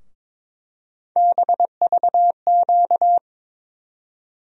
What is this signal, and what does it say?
Morse code, heard as a keyed tone, about 22 wpm, 700 Hz. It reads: BVQ
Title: USA Call Sign Postfixes - 3 Letters 22wpm